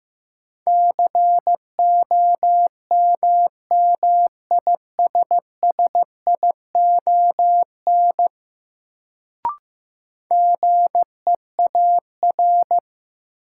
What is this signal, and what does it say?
Morse code, heard as a keyed tone, about 15 wpm, 700 Hz. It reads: COMMISSION GEAR
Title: Sentences from Top 1400 Words - New Word Drill Alt - 15wpm